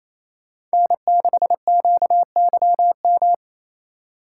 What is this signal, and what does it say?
Morse code, heard as a keyed tone, about 35 wpm, 700 Hz. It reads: N6QYM
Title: Speed Racing - USA Call Signs 28wpm